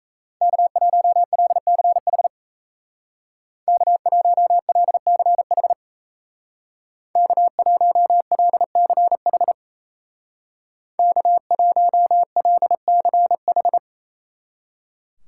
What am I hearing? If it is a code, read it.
Morse code, heard as a keyed tone, about 35 wpm, 700 Hz. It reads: K1LCH K1LCH K1LCH K1LCH
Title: Speed Racing - USA 1x3 Call Signs 28wpm